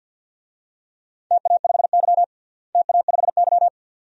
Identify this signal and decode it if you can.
Morse code, heard as a keyed tone, about 51 wpm, 700 Hz. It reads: TA5Y TA5Y
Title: DXCC-100 Call Signs - 4 Characters - Repeated 2x - 50wpm